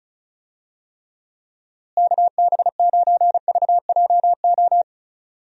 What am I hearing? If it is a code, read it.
Morse code, heard as a keyed tone, about 34 wpm, 700 Hz. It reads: KB9VJO
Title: USA Call Signs - 6 Characters - Repeated 3x - 35wpm